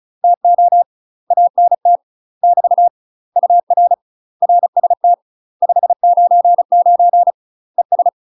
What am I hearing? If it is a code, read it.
Morse code, heard as a keyed tone, about 36 wpm, 700 Hz. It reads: TO ANT <BT> UR RST 599 ES
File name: UK MASTER QSOs TRAINING / 35WPM / UK MASTER QSOs 0017